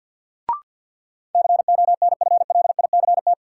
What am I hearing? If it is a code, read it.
Morse code, heard as a keyed tone, about 53 wpm, 700 Hz. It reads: CONFLICT
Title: Sentences from Top 1300 Words - New Word Drill Alt - 50wpm